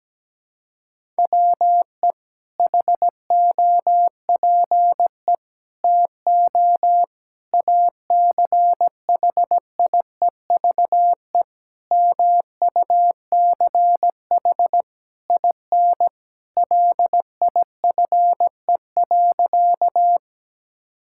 Morse code, heard as a keyed tone, about 17 wpm, 700 Hz. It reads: WE HOPE TO ACHIEVE MUCH IN LIFE.
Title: Sentences from Top 1000 Words - Review Part 6 - 17wpm